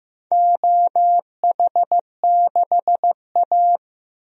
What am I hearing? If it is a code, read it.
Morse code, heard as a keyed tone, about 15 wpm, 700 Hz. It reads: OH6A
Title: DXCC-100 Call Signs - 4 Characters - Repeated 2x - 15wpm